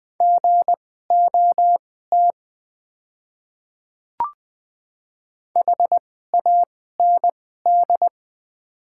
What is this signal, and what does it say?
Morse code, heard as a keyed tone, about 20 wpm, spaced as for 14 wpm, 700 Hz. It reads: GOT HAND
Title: 2x Character Spacing - Top 300 Words - 20wpm